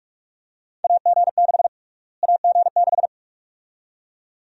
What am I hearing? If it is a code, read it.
Morse code, heard as a keyed tone, about 46 wpm, 700 Hz. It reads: AG6 AG6
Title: USA Call Sign Prefixes - Repeated 2x - 45wpm